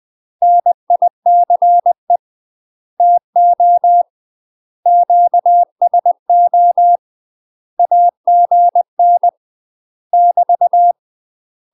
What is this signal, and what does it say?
Morse code, heard as a keyed tone, about 20 wpm, 700 Hz. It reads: NICE TO QSO AGN <BT>
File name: UK MASTER QSOs TRAINING / 20WPM / UK MASTER QSOs 0063